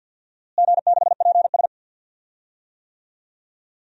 Morse code, heard as a keyed tone, about 50 wpm, 700 Hz. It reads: G6PS
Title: DXCC-100 Call Signs - 4 Characters - Repeated 2x - 50wpm